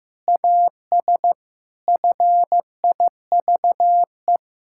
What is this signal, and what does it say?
Morse code, heard as a keyed tone, about 15 wpm, 700 Hz. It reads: AS FIVE
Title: Pangram Challenge v2 - 15wpm